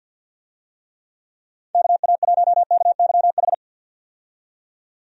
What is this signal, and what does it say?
Morse code, heard as a keyed tone, about 49 wpm, 700 Hz. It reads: KA1KYH
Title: USA Call Signs - 6 Characters - 50wpm